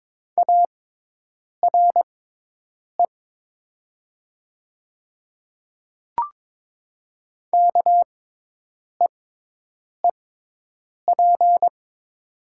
Morse code, heard as a keyed tone, about 22 wpm, spaced as for 8 wpm, 700 Hz. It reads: ARE KEEP
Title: 6x Character Spacing - Top 300 Words - 22wpm